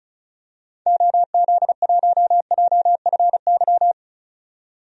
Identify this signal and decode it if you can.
Morse code, heard as a keyed tone, about 34 wpm, 700 Hz. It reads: OZ1JFY
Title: DXCC-100 Call Signs - 6 Characters - 35wpm